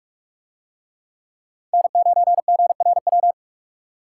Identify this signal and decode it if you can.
Morse code, heard as a keyed tone, about 48 wpm, 700 Hz. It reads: N9GRW